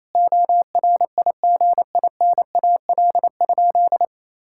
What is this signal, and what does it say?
Morse code, heard as a keyed tone, about 28 wpm, 700 Hz. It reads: ORIGINAL?
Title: Sentences from Top 1000 Words - Review Part 6 - 28wpm